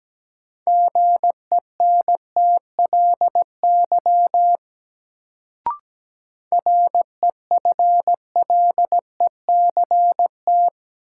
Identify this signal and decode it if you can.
Morse code, heard as a keyed tone, about 17 wpm, 700 Hz. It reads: GENTLY REFLECT
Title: Sentences from Top 1500 Words - New Word Drill - 17wpm